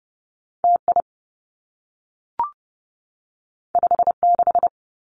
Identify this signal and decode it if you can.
Morse code, heard as a keyed tone, about 30 wpm, 700 Hz. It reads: T I 5 6
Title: Two Challenging Letter-Number Combinations 30wpm